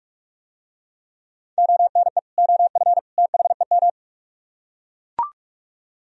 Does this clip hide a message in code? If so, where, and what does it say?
Yes: Morse code from 1.6 to 3.9 s, heard as a keyed tone, about 44 wpm, 700 Hz. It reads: ONE OF THEM